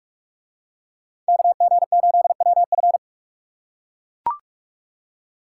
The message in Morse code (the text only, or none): KG8WF